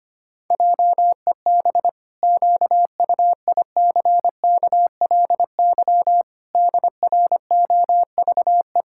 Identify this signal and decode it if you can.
Morse code, heard as a keyed tone, about 25 wpm, 700 Hz. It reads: JEB QUICKLY DROVE
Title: Pangram Challenge v2 - 25wpm